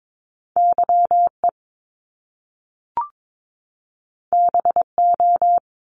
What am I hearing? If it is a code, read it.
Morse code, heard as a keyed tone, about 22 wpm, 700 Hz. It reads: YE BO
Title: Top Two Letter Combinations in Words 22wpm